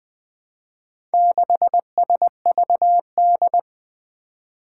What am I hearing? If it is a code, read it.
Morse code, heard as a keyed tone, about 20 wpm, 700 Hz. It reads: 6SVD
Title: Four Challenging Letter-Number Combinations 20wpm